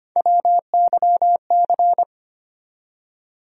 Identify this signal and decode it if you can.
Morse code, heard as a keyed tone, about 25 wpm, 700 Hz. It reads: WYC
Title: USA Call Sign Postfixes - 3 Letters 25wpm